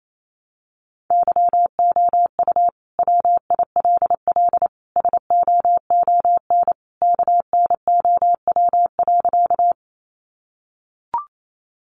Morse code, heard as a keyed tone, about 28 wpm, 700 Hz. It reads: YOU WILL SOON KNOW.
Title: Sentences from Top 400 Words 28wpm